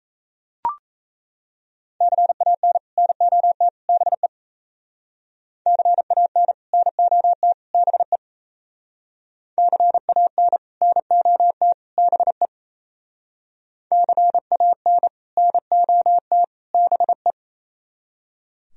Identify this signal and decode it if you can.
Morse code, heard as a keyed tone, about 35 wpm, 700 Hz. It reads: CAN NOT BE CAN NOT BE CAN NOT BE CAN NOT BE